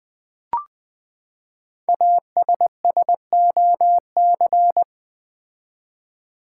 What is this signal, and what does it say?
Morse code, heard as a keyed tone, about 20 wpm, 700 Hz. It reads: ASSOC
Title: Build A Word - 5 Letters At A Time - 10-Letter Words - 20wpm